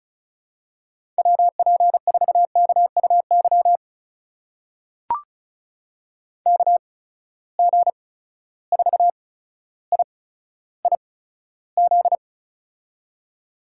Morse code, heard as a keyed tone, about 36 wpm, spaced as for 15 wpm, 700 Hz. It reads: WP4KUY KG4IIZ
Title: USA Call Signs - 6 Characters - 8x Character Spacing - 35wpm